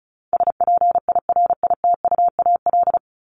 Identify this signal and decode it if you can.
Morse code, heard as a keyed tone, about 34 wpm, 700 Hz. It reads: SPIRITUAL
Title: Top 1501-1750 Words 35wpm